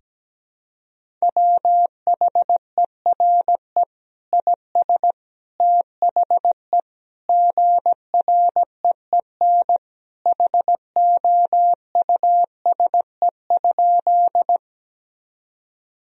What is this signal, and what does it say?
Morse code, heard as a keyed tone, about 17 wpm, 700 Hz. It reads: WHERE IS THE GREEN HOUSE?
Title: Sentences from Top 1000 Words - Review Part 2 - 17wpm